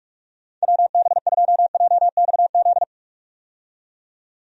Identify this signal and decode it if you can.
Morse code, heard as a keyed tone, about 46 wpm, 700 Hz. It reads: WB2JXZ
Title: USA Call Signs - 6 Characters - 45wpm